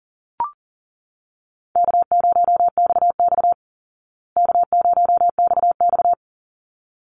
Morse code, heard as a keyed tone, about 40 wpm, 700 Hz. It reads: K0XX K0XX
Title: Easy Call Signs Repeated 2x - 40wpm